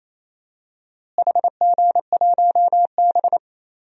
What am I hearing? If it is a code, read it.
Morse code, heard as a keyed tone, about 28 wpm, 700 Hz. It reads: HG1B
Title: DXCC-100 Call Signs - 4 Characters - Repeated 3x - 28wpm